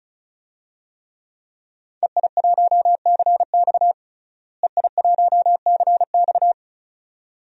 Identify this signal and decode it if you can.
Morse code, heard as a keyed tone, about 36 wpm, 700 Hz. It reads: EI1CX EI1CX
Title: DXCC-100 Call Signs - 5 Characters - Repeated 2x - 35wpm